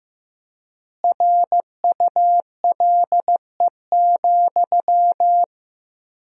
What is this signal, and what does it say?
Morse code, heard as a keyed tone, about 15 wpm, 700 Hz. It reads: RULE,
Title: 4x Word-Spacing - Sentences from Top 400 Words - 15wpm